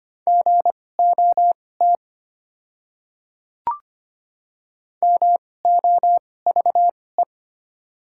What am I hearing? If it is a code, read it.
Morse code, heard as a keyed tone, about 25 wpm, spaced as for 18 wpm, 690 Hz. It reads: GOT MOVE
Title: 2x Character Spacing - Top 500 Words - 25wpm